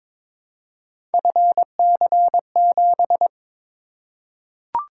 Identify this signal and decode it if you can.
Morse code, heard as a keyed tone, about 23 wpm, 700 Hz. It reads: FC7
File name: practice / 22wpm / Three Letter-Number Combinations - 22wpm